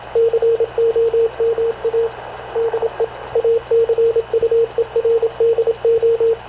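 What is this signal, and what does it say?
Morse code, heard as a keyed tone, about 26 wpm, 460 Hz, with radio noise. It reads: COMA DE ACUERDO
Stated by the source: Havana Radio / CLA sending Spanish press (PX) in Morse, recorded by Radio Officer Paul Dumesnil onboard CNAV QUEST/CZDO.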